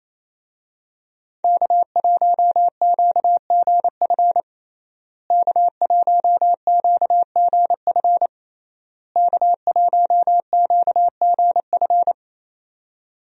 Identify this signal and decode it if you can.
Morse code, heard as a keyed tone, about 27 wpm, 700 Hz. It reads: K1QGF K1QGF K1QGF
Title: USA Call Signs - 5 Characters - Repeated 3x - 28wpm